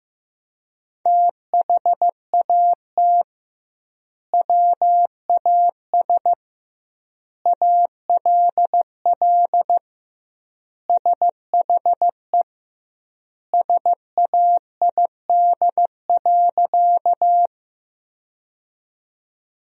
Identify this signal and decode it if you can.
Morse code, heard as a keyed tone, about 15 wpm, 700 Hz. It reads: THAT WAS ALL SHE SAID.
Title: 2x Word-Spacing - Sentences from Top 100 Words - 15wpm